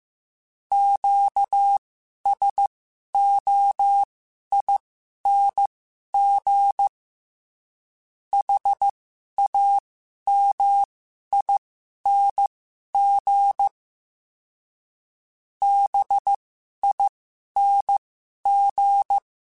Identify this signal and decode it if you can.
Morse code, heard as a keyed tone, about 15 wpm, spaced as for 11 wpm, 800 Hz. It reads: QSOING HAMING BING
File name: Special INGs-mp3s / Special INGs-mp3s / Sngl Ing @10